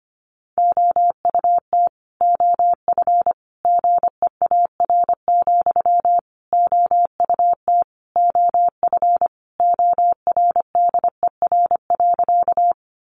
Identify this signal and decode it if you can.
Morse code, heard as a keyed tone, about 25 wpm, 700 Hz. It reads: OUT OF GEAR, OUT OF ORDER.
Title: Sentences from Top 1400 Words 25wpm